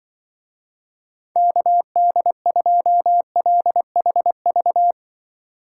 Morse code, heard as a keyed tone, about 24 wpm, 700 Hz. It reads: KD2LHV
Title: Speed Racing - USA 2x3 Call Signs 20wpm